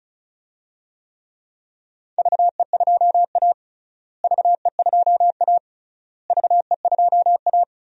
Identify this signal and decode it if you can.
Morse code, heard as a keyed tone, about 34 wpm, 700 Hz. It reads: VE2A VE2A VE2A
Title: DXCC-100 Call Signs - 4 Characters - Repeated 3x - 35wpm